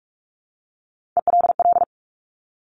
Morse code, heard as a keyed tone, about 46 wpm, 700 Hz. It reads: ELL
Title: Top Three Letter Combinations in Words 45wpm